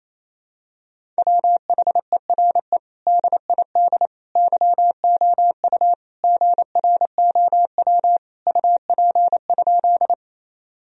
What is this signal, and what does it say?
Morse code, heard as a keyed tone, about 28 wpm, 700 Hz. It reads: WHERE DID YOU GROW UP?
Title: Sentences from Top 1000 Words - Review Part 1 - 28wpm